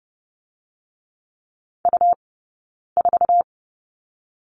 Morse code, heard as a keyed tone, about 30 wpm, 700 Hz. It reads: U 4
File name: course / 30:10wpm / 099 Lesson 17d - Letter u - Two Characters 30wpm